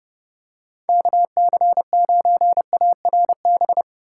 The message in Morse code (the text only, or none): KC9ARB